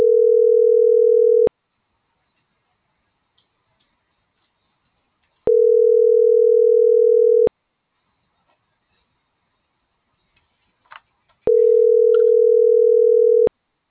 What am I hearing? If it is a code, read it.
Ringback tone (North America), 440 + 480 Hz.
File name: CENTRAL TERRESTRE / out cris.f